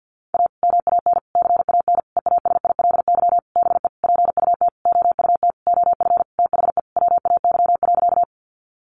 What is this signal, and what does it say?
Morse code, heard as a keyed tone, about 50 wpm, 700 Hz. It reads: A MAN CAN EASILY BE PUT OUT OF THE WAY.
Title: Sentences from Top 900 Words 50wpm